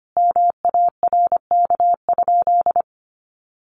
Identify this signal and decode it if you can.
Morse code, heard as a keyed tone, about 25 wpm, 700 Hz. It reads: MARK?
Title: Sentences from Top 300 Words 25wpm